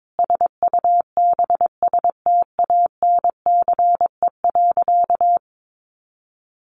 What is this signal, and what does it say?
Morse code, heard as a keyed tone, about 22 wpm, 700 Hz. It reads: SUBSTANCE.
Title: Sentences from Top 1500 Words 22wpm